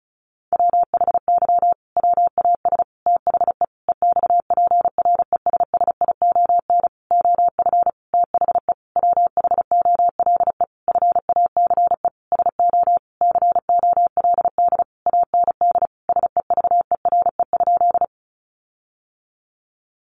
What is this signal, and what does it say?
Morse code, heard as a keyed tone, about 34 wpm, 700 Hz. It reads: WHY WAS THE EXPRESSION OF THE WHOLE FACE SO COLD AND SEVERE?
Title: Sentences from Top 1400 Words 35wpm